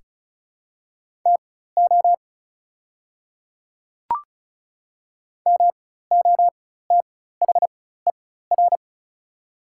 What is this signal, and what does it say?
Morse code, heard as a keyed tone, about 36 wpm, spaced as for 16 wpm, 700 Hz. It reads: TO MOTHER
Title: 4x Character Spacing - Top 300 Words - 35wpm